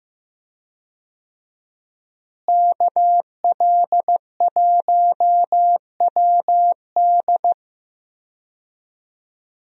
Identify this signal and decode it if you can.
Morse code, heard as a keyed tone, about 15 wpm, 700 Hz. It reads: KL1WD
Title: USA Call Signs - 5 Characters - Repeated 2x - 15wpm